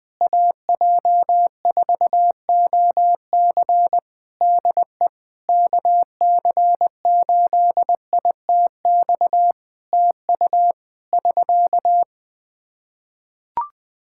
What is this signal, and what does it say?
Morse code, heard as a keyed tone, about 21 wpm, 700 Hz. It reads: AJ4OC DE KC8ITX TU <SK>
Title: Practice QSO - Exchange 4 - Ending - 20wpm